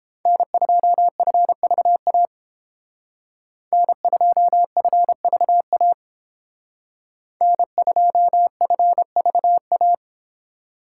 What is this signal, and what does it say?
Morse code, heard as a keyed tone, about 30 wpm, 700 Hz. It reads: N2FVA N2FVA N2FVA